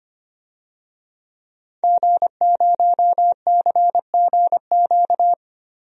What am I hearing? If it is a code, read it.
Morse code, heard as a keyed tone, about 25 wpm, 700 Hz. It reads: G0CGQ
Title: DXCC-100 Call Signs - 5 Characters - Repeated 3x - 25wpm